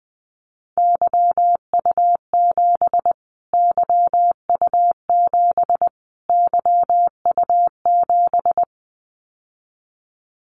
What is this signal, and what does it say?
Morse code, heard as a keyed tone, about 20 wpm, 700 Hz. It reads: YU7 YU7 YU7
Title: Callsign Prefixes for Top 25 DXCC - Repeated 3x - 20wpm